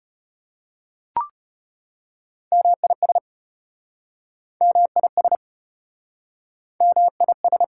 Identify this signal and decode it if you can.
Morse code, heard as a keyed tone, about 33 wpm, 700 Hz. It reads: MIS MIS MIS